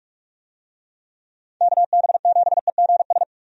Morse code, heard as a keyed tone, about 44 wpm, 700 Hz. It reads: KB7EGS
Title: USA Call Signs - 6 Characters - 45wpm